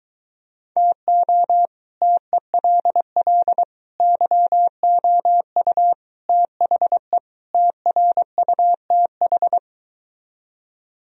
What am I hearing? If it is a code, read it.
Morse code, heard as a keyed tone, about 23 wpm, 700 Hz. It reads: TO TELL YOU THE TRUTH